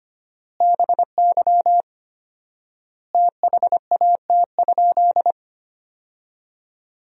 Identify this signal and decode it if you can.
Morse code, heard as a keyed tone, about 25 wpm, 700 Hz. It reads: BY THAT?
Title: 4x Word-Spacing - Sentences from Top 200 Words - 25wpm